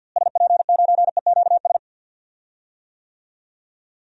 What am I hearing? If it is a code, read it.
Morse code, heard as a keyed tone, about 48 wpm, 700 Hz. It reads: SP9EQS
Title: DXCC-100 Call Signs - 6 Characters - Repeated 3x - 50wpm